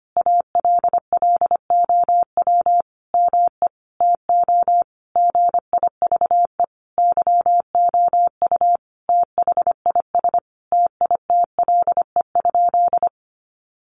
Morse code, heard as a keyed tone, about 25 wpm, 700 Hz. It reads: ALLOW ME TO GIVE YOU THIS TITLE?
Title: Sentences from Top 1200 Words 25wpm